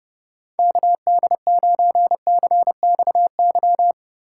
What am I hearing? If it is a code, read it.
Morse code, heard as a keyed tone, about 30 wpm, 700 Hz. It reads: KD9CXY
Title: USA Call Signs - 6 Characters - 3x Character Spacing - 30wpm